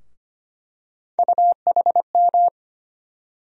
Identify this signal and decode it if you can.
Morse code, heard as a keyed tone, about 25 wpm, 700 Hz. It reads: UHM
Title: USA Call Sign Postfixes - 3 Letters 25wpm